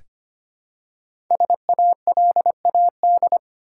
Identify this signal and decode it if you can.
Morse code, heard as a keyed tone, about 25 wpm, 700 Hz. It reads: SALAD
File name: 5 Letter Words - First 3 Letters Given - 25wpm